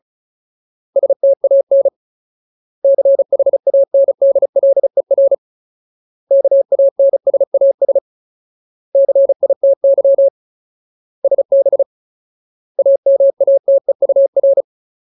Morse code, heard as a keyed tone, about 34 wpm, 550 Hz. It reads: STAN CHANDLER KANSAS CITY SB AMATEUR
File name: Ultimate Word List / 35WPM Ultimate